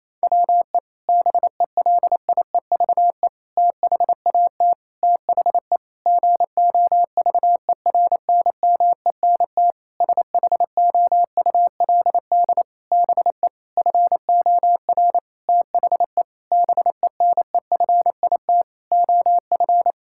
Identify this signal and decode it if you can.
Morse code, heard as a keyed tone, about 28 wpm, 700 Hz. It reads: WE BELIEVE THAT THE GOVERNMENT SHOULD BE FOR THE BENEFIT OF
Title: Sentences from Top 1000 Words - Review Part 6 - 28wpm